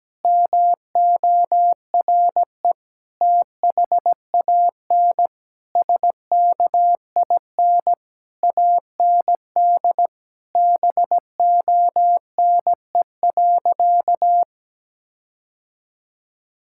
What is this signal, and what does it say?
Morse code, heard as a keyed tone, about 17 wpm, 700 Hz. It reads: MORE THAN SKIN AND BONE.
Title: Sentences from Top 1600 Words 17wpm